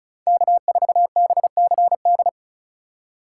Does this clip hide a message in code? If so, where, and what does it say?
Yes: Morse code from 0.3 to 2.3 s, heard as a keyed tone, about 44 wpm, 700 Hz. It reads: K4BCD